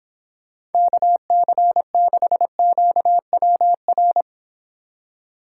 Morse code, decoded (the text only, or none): KC6QWR